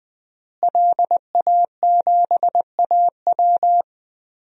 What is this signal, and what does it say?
Morse code, heard as a keyed tone, about 20 wpm, 700 Hz. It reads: LA7AW
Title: DXCC-100 Call Signs - 5 Characters - Repeated 3x - 20wpm